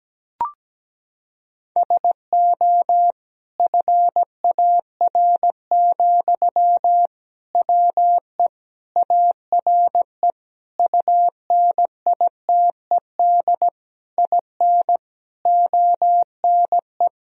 Morse code, heard as a keyed tone, about 17 wpm, 700 Hz. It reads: SO FAR, WE ARE UNITED IN ONE
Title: Sentences from Top 1000 Words - Review Part 4 - 17wpm